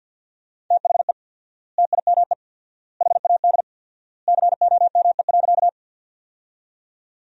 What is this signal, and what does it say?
Morse code, heard as a keyed tone, about 50 wpm, 700 Hz. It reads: THE TIME HAD COME.
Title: 4x Word-Spacing - Sentences from Top 100 Words - 50wpm